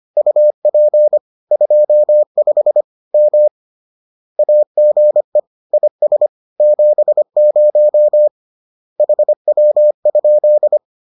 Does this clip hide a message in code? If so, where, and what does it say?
Yes: Morse code from 0.2 to 10.8 s, heard as a keyed tone, about 25 wpm, 600 Hz. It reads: UP 25 M AGE IS 70 HW?